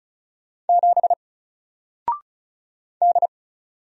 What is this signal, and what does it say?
Morse code, heard as a keyed tone, about 33 wpm, 700 Hz. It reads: 7 D
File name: Single-Letter-Number-35wpm